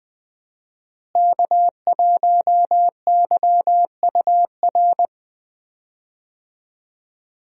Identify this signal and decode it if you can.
Morse code, heard as a keyed tone, about 20 wpm, 700 Hz. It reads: K1YUR
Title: USA Call Signs - 5 Characters - Repeated 3x - 20wpm